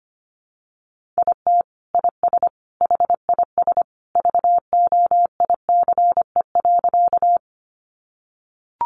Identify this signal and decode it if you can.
Morse code, heard as a keyed tone, about 25 wpm, 700 Hz. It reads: IT IS HIS VOICE.
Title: Sentences from Top 500 Words 25wpm